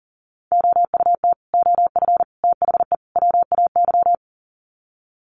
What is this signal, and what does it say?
Morse code, heard as a keyed tone, about 40 wpm, 700 Hz. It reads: OUT OF THE WAY
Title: Sets of 4 Words 40wpm